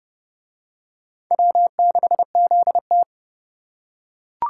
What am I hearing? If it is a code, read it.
Morse code, heard as a keyed tone, about 27 wpm, 700 Hz. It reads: W6ZT